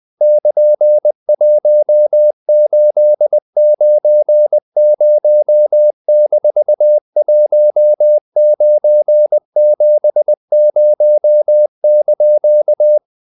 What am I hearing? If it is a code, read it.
Morse code, heard as a keyed tone, about 20 wpm, 600 Hz. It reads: <KN>1890-1970)
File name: CW / Practice / EnglishQuotations0320